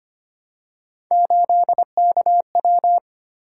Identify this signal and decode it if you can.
Morse code, heard as a keyed tone, about 25 wpm, 700 Hz. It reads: 8KW
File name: Three Letter-Number Combinations - 25wpm